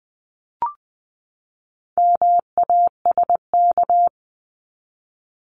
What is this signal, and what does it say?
Morse code, heard as a keyed tone, about 20 wpm, 700 Hz. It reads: MASK
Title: 260 Lesson 42e - Words 20-20wpm